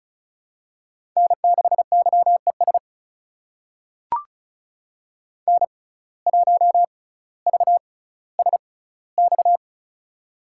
Morse code, heard as a keyed tone, about 36 wpm, spaced as for 18 wpm, 700 Hz. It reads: N6YES N1VSX
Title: USA Call Signs - 5 Characters - 6x Character Spacing - 35wpm